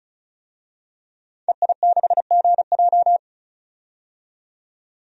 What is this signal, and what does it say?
Morse code, heard as a keyed tone, about 35 wpm, 700 Hz. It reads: EI6GJ
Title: DXCC-100 Call Signs - 5 Characters - 35wpm